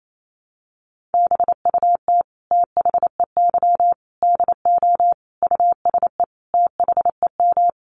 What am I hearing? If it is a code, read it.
Morse code, heard as a keyed tone, about 28 wpm, 700 Hz. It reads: BUT THEY DO USE THEM
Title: Sentences from Top 1300 Words 28wpm